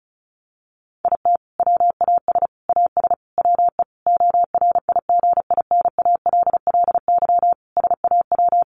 Morse code, heard as a keyed tone, about 34 wpm, 700 Hz. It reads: IT WAS AS WE ORIGINALLY SAW
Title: Sentences from Top 1500 Words 35wpm